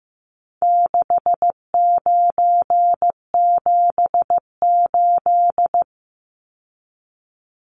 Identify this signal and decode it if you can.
Morse code, heard as a keyed tone, about 15 wpm, 700 Hz. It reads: 6978
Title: Numbers - 4-Digit 15wpm